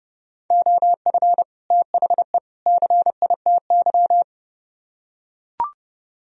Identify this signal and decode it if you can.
Morse code, heard as a keyed tone, about 29 wpm, 700 Hz. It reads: OF THE CITY